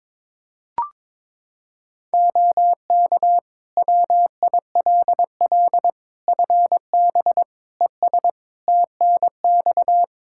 Morse code, heard as a keyed tone, about 22 wpm, 700 Hz. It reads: OK WILL FB ES TNX
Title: Practice QSO - Exchange 2 - Info - 22wpm